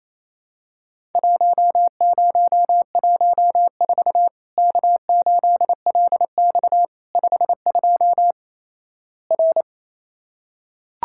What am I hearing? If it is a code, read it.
Morse code, heard as a keyed tone, about 28 wpm, 700 Hz. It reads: 1014 K8LX 52 R
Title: Worked All Europe DX Contest - QTC Exchange - 28wpm